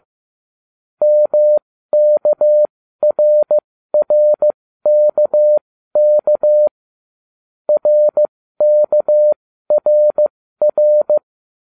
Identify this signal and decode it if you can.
Morse code, heard as a keyed tone, about 15 wpm, spaced as for 13 wpm, 600 Hz. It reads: MKRRKK RKRR